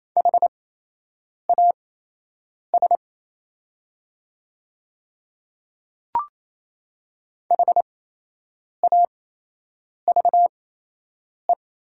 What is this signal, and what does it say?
Morse code, heard as a keyed tone, about 28 wpm, spaced as for 8 wpm, 700 Hz. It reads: HAS HAVE
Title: 8x Character Spacing - Top 100 Words - 28wpm